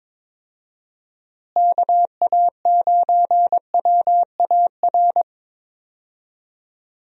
Morse code, heard as a keyed tone, about 21 wpm, 700 Hz. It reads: KA9WAR
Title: USA Call Signs - 6 Characters - Repeated 2x - 22wpm